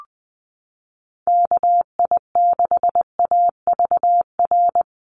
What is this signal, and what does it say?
Morse code, heard as a keyed tone, about 20 wpm, 700 Hz. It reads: KI6AVR
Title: USA Call Signs Repeated 2x 20wpm